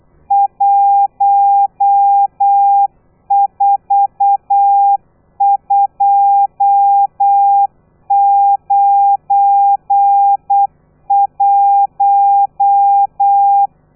Morse code, heard as a keyed tone, about 8 wpm, 800 Hz. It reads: 14291
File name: telegrafia / i2tfj / Lez12 35